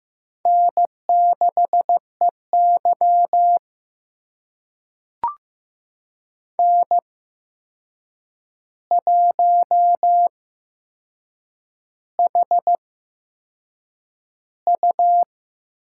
Morse code, heard as a keyed tone, about 15 wpm, spaced as for 6 wpm, 700 Hz. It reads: N6EY N1HU
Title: USA Call Signs - 4 Characters - 8x Character Spacing - 15wpm